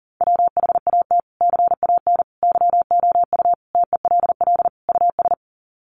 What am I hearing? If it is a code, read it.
Morse code, heard as a keyed tone, about 40 wpm, 700 Hz. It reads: WHAT CAN YOU TELL US
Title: Sets of 5 Words Encore 40wpm